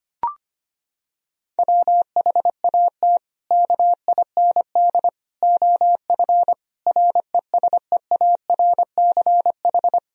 Morse code, heard as a keyed tone, about 25 wpm, 700 Hz. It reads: WHAT KIND OF RESEARCH
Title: Sentences from Top 1000 Words - Review Part 4 - 25wpm